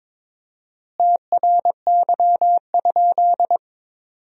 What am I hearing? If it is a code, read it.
Morse code, heard as a keyed tone, about 22 wpm, 700 Hz. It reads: TRY?
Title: 4x Word-Spacing - Sentences from Top 200 Words - 22wpm